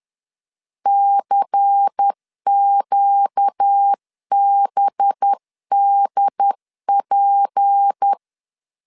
Morse code, heard as a keyed tone, about 10 wpm, 800 Hz. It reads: CQBDP
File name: CW / DET / 1 6 1500C S C 10wpm 4252